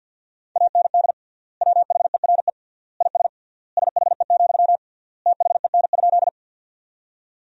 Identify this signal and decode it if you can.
Morse code, heard as a keyed tone, about 50 wpm, 700 Hz. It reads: AND WHERE IS SHE, THEN?
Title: 3x Word-Spacing - Sentences from Top 200 Words - 50wpm